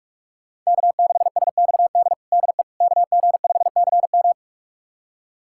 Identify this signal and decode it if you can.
Morse code, heard as a keyed tone, about 47 wpm, 700 Hz. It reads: K6SXD DE KG5CM
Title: USA Callsign DE Callsign - 5 Characters - 45wpm